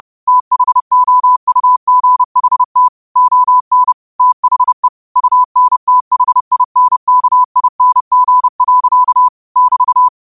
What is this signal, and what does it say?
Morse code, heard as a keyed tone, about 30 wpm, 1000 Hz. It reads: THOUGHT ON THE UNTHINKING. <BT>
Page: Quotes for Sun, 17 Aug 2025 in Morse Code at 30 words per minute.